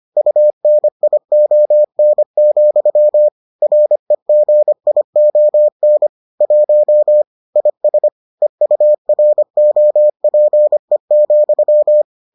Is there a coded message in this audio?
Yes: Morse code from 0.2 to 12.0 s, heard as a keyed tone, about 25 wpm, 600 Hz. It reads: UNION, REGION 1 IS EUROPE,